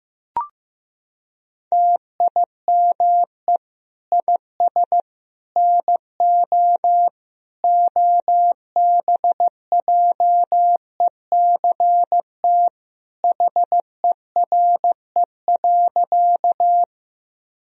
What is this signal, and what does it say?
Morse code, heard as a keyed tone, about 15 wpm, 700 Hz. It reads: TIME IS NO OBJECT HERE.
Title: Sentences from Top 1000 Words - Review Part 3 - 15wpm